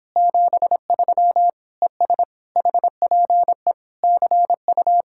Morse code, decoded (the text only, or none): 73 ES HPE CU